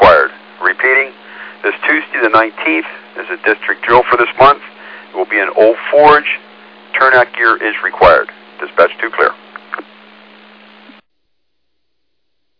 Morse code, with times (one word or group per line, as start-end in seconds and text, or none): none